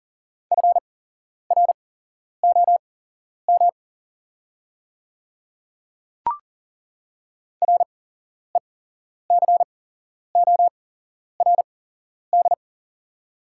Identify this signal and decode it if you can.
Morse code, heard as a keyed tone, about 40 wpm, spaced as for 11 wpm, 700 Hz. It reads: FROM RECORD
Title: 8x Character Spacing - Top 500 Words - 40wpm